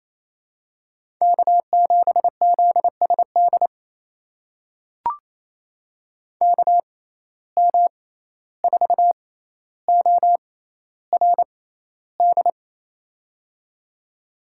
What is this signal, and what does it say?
Morse code, heard as a keyed tone, about 27 wpm, spaced as for 14 wpm, 700 Hz. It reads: K7ZSD KM4ORD
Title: USA Call Signs - 6x Character Spacing - 28wpm